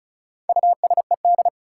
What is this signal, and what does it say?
Morse code, heard as a keyed tone, about 36 wpm, 700 Hz. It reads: USED
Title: Sentences from Top 600 Words - New Word Drill - 35wpm